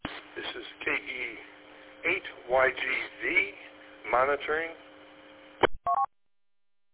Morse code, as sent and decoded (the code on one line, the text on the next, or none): none